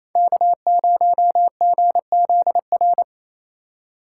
Morse code, heard as a keyed tone, about 28 wpm, 700 Hz. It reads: K0GZR
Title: USA Call Signs - 3x Character Spacing - 28wpm